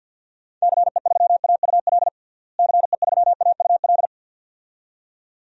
Morse code, heard as a keyed tone, about 51 wpm, 700 Hz. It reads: CE3AUL CE3AUL
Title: DXCC-100 Call Signs - 6 Characters - Repeated 2x - 50wpm